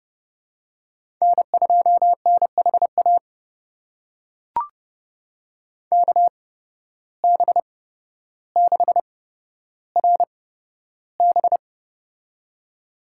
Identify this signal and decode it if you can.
Morse code, heard as a keyed tone, about 30 wpm, spaced as for 13 wpm, 700 Hz. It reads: N2NHA KB6RB
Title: USA Call Signs - 5 Characters - 8x Character Spacing - 30wpm